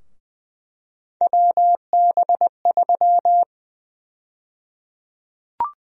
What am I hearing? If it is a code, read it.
Morse code, heard as a keyed tone, about 20 wpm, 700 Hz. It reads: WB3
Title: USA Call Sign Prefixes - 20wpm